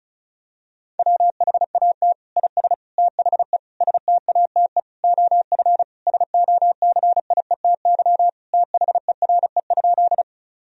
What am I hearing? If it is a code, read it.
Morse code, heard as a keyed tone, about 34 wpm, 700 Hz. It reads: WHAT IS THE STATE OF SOCIETY THERE?
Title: Sentences from Top 1000 Words - Review Part 4 - 35wpm